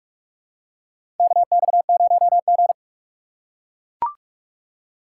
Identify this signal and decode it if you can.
Morse code, heard as a keyed tone, about 42 wpm, 700 Hz. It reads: KX0G
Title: USA Call Signs - 2x Character Spacing - 45wpm